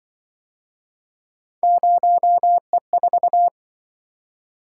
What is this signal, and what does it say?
Morse code, heard as a keyed tone, about 24 wpm, 700 Hz. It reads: 0E4
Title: Speed-Racing - Three Letter-Number Combinations - 20wpm